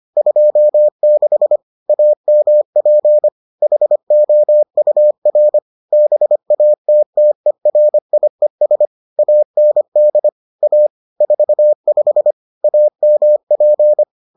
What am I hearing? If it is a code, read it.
Morse code, heard as a keyed tone, about 25 wpm, 600 Hz. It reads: 26 AMP HOUR BATTERIES AND A 45 AMP